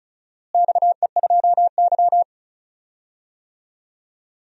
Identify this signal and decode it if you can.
Morse code, heard as a keyed tone, about 36 wpm, 700 Hz. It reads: XE2Y
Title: DXCC-100 Call Signs - 4 Characters - Repeated 2x - 35wpm